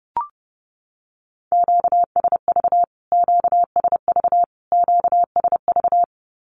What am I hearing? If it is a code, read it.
Morse code, heard as a keyed tone, about 30 wpm, 700 Hz. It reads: QSV QSV QSV
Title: Q-Codes Repeated 3x 30wpm